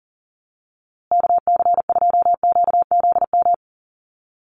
Morse code, heard as a keyed tone, about 38 wpm, 700 Hz. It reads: KC2QZM
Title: USA Call Signs Repeated 2x 40wpm